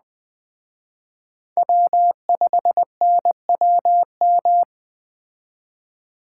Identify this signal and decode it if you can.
Morse code, heard as a keyed tone, about 20 wpm, 700 Hz. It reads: W5NWM
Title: Speed Racing - USA 1x3 Call Signs 17wpm